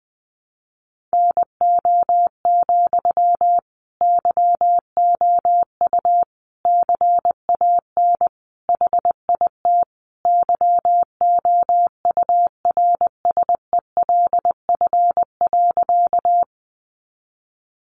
Morse code, heard as a keyed tone, about 20 wpm, 700 Hz. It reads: NO, YOU CAN HIT YOURSELF.
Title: Sentences from Top 1200 Words 20wpm